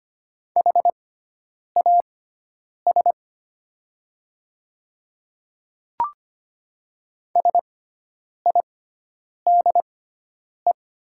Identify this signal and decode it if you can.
Morse code, heard as a keyed tone, about 25 wpm, spaced as for 9 wpm, 700 Hz. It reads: HAS SIDE
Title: 6x Character Spacing - Top 300 Words - 25wpm